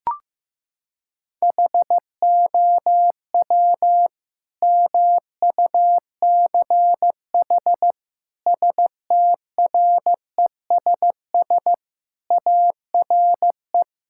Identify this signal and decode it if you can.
Morse code, heard as a keyed tone, about 15 wpm, 700 Hz. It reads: HOW MUCH STRESS ARE
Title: Sentences from Top 1000 Words - Review Part 6 - 15wpm